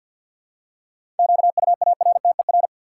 Morse code, heard as a keyed tone, about 53 wpm, 700 Hz. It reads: QUARTER
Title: Seven Letter Words - 50wpm